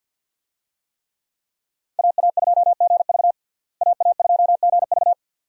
Morse code, heard as a keyed tone, about 52 wpm, 700 Hz. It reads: AA2GV AA2GV
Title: USA Call Signs - 5 Characters - Repeated 2x - 50wpm